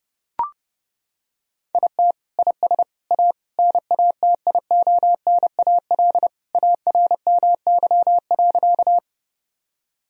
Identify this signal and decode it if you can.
Morse code, heard as a keyed tone, about 30 wpm, 700 Hz. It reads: IT IS A NATIONAL ARMY.
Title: Sentences from Top 1000 Words - Review Part 4 - 30wpm